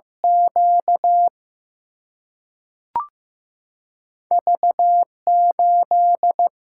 Morse code, heard as a keyed tone, about 15 wpm, 700 Hz. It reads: Q V 8
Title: Single-Letter Single-Number 15wpm